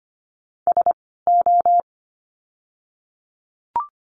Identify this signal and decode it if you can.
Morse code, heard as a keyed tone, about 25 wpm, 700 Hz. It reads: S O
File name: course / 25:10wpm / 094 Lesson 16f - Letter c - Words 25wpm